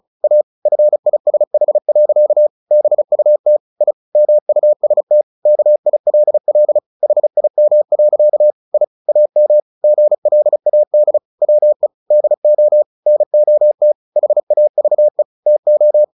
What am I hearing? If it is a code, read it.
Morse code, heard as a keyed tone, about 34 wpm, 600 Hz. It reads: A FISH. BUT I MUST KILL HIM. I AM GLAD WE DO NOT HAVE TO
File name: the old man and the sea 2 35wpm